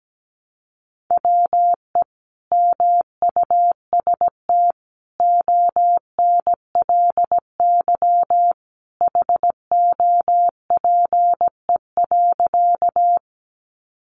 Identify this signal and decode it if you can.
Morse code, heard as a keyed tone, about 17 wpm, 700 Hz. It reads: WE MUST ONLY HOPE.
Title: Sentences from Top 1000 Words 17wpm